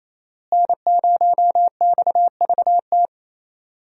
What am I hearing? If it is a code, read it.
Morse code, heard as a keyed tone, about 28 wpm, 700 Hz. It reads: N0XVT
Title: USA Call Signs - 5 Characters - Repeated 2x - 28wpm